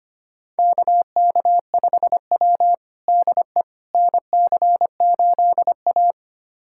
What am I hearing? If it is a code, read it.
Morse code, heard as a keyed tone, about 25 wpm, 700 Hz. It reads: KK5W DE NC8A
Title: USA Callsign DE Callsign - 4 Characters - 25wpm